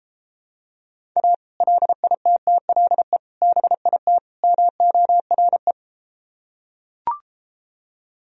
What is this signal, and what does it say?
Morse code, heard as a keyed tone, about 33 wpm, 700 Hz. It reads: A LITTLE BIT MORE